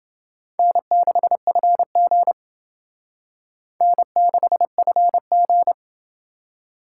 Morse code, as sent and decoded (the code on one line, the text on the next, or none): -. -.... ..-. --. / -. -.... ..-. --.
N6FG N6FG